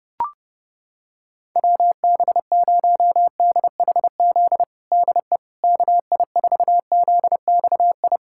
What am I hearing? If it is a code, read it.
Morse code, heard as a keyed tone, about 31 wpm, 700 Hz. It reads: WB0DHZ DE KI4ZXI
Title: USA Callsign DE Callsign - 6 Characters - 30wpm